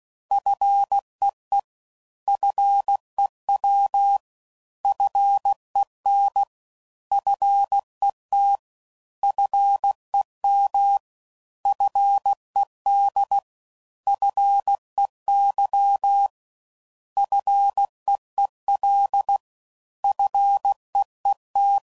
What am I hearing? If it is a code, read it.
Morse code, heard as a keyed tone, about 16 wpm, 800 Hz. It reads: FEE FEW FEN FET FEM FED FEY FEEL FEET